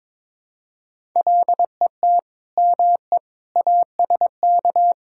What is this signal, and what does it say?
Morse code, heard as a keyed tone, about 22 wpm, 700 Hz. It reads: LET ME ASK
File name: practice / 22wpm / Build A Phrase - 5-Word Phrases - 22wpm